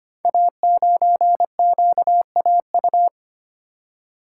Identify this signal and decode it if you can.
Morse code, heard as a keyed tone, about 25 wpm, 700 Hz. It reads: A9QAU
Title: Code Groups - 5 Letter:Numbers - 25wpm